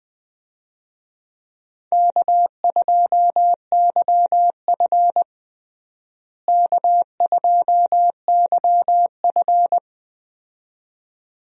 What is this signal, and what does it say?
Morse code, heard as a keyed tone, about 20 wpm, 700 Hz. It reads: K2YF K2YF
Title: USA Call Signs - 4 Characters - Repeated 2x - 20wpm